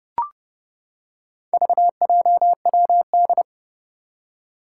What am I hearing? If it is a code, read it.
Morse code, heard as a keyed tone, about 28 wpm, 700 Hz. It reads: VJWD
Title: Four Challenging Letter-Number Combinations 30wpm